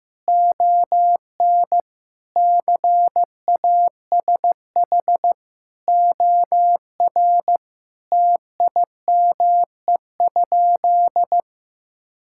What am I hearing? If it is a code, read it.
Morse code, heard as a keyed tone, about 15 wpm, 700 Hz. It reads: ON CASH OR TIME?
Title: Sentences from Top 1000 Words - Review Part 5 - 15wpm